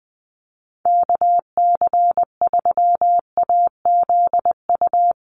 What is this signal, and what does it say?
Morse code, heard as a keyed tone, about 19 wpm, 700 Hz. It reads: KC3AZU
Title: USA Call Signs Repeated 2x 20wpm